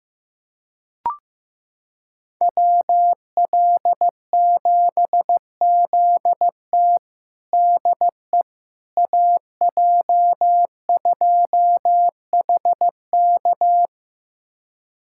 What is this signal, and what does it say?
Morse code, heard as a keyed tone, about 15 wpm, 700 Hz. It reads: WL7ZT DE AJ2HK
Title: USA Callsign DE Callsign - 5 Characters - 15wpm